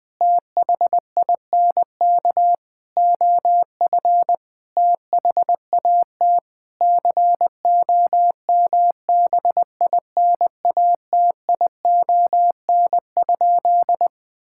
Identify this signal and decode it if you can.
Morse code, heard as a keyed tone, about 21 wpm, 700 Hz. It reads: THINK OF THAT COMBINATION?
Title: Sentences from Top 1000 Words - Review Part 6 - 20wpm